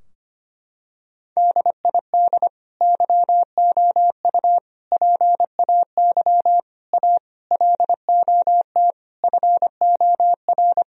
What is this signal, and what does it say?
Morse code, heard as a keyed tone, about 25 wpm, 700 Hz. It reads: DID YOU PAY A LOT FOR
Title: Sentences from Top 1000 Words - Review Part 4 - 25wpm